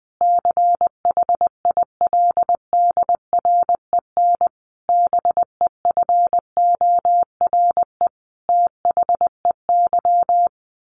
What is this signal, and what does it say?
Morse code, heard as a keyed tone, about 20 wpm, 700 Hz. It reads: CHILDREN BEFORE THEY
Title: Sentences from Top 1100 Words 20wpm